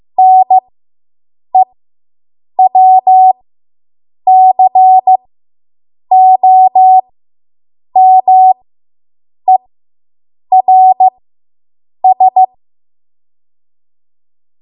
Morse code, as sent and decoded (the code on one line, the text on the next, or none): -. .-- -.-. --- -- . .-. ...
NEWCOMERS